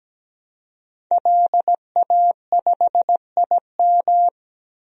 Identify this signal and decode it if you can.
Morse code, heard as a keyed tone, about 17 wpm, 700 Hz. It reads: LA5IM
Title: DXCC-100 Call Signs - 5 Characters - 17wpm